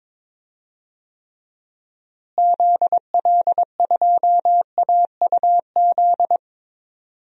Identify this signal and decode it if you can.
Morse code, heard as a keyed tone, about 22 wpm, 700 Hz. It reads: ZL2AUZ
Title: DXCC-100 Call Signs - 6 Characters - Repeated 3x - 22wpm